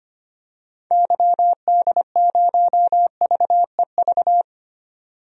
Morse code, heard as a keyed tone, about 25 wpm, 700 Hz. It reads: YD0VEV
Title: DXCC-100 Call Signs - 6 Characters - 25wpm